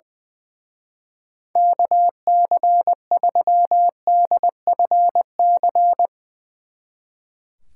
Morse code, heard as a keyed tone, about 22 wpm, 700 Hz. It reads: KC3DFC
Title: Speed Racing - USA 2x3 Call Signs 20wpm